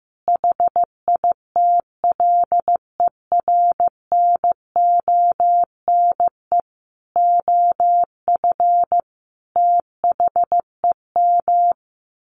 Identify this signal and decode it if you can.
Morse code, heard as a keyed tone, about 15 wpm, 700 Hz. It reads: HITLER NONE OF THEM
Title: A Sound Of Thunder 15wpm